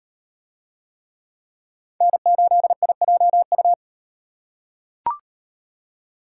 Morse code, heard as a keyed tone, about 36 wpm, 700 Hz. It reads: N8IJU